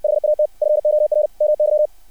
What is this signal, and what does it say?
Morse code, heard as a keyed tone, about 60 wpm, 600 Hz. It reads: HAT FOR MY
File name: archived / CRSnet